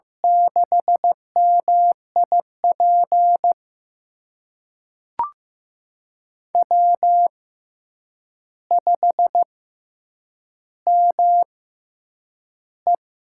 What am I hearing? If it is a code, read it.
Morse code, heard as a keyed tone, about 15 wpm, spaced as for 8 wpm, 700 Hz. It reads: 6MIP W5ME
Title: USA Call Signs - 6x Character Spacing - 15wpm